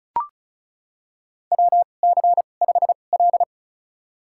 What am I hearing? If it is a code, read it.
Morse code, heard as a keyed tone, about 36 wpm, spaced as for 25 wpm, 700 Hz. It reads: WC5L
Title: USA Call Signs - 4 Characters - 2x Character Spacing - 35wpm